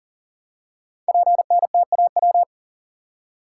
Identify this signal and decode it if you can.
Morse code, heard as a keyed tone, about 38 wpm, 700 Hz. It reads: PNTAW
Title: Code Groups - 5 Letters - 40wpm